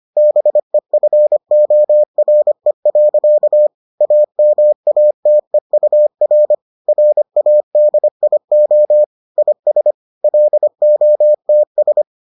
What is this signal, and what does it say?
Morse code, heard as a keyed tone, about 25 wpm, 600 Hz. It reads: BEFORE. AMATEUR RADIO IS LOTS